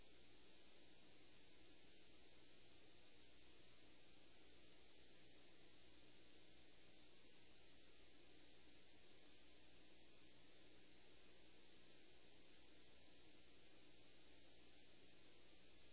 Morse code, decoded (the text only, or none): none